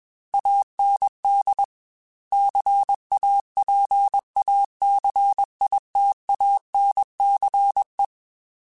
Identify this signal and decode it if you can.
Morse code, heard as a keyed tone, about 21 wpm, 800 Hz. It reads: AND CAPACITANCE
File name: Technical Subjects / Technical Subjects-mp3s / Reactance @20